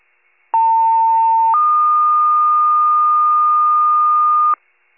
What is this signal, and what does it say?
Two-tone page, heard as a one-second tone, then a three-second tone, about 900 then 1220 Hz. It is heard from 0.5 to 4.5 s.
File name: pager / 73 RESCUE / wcalert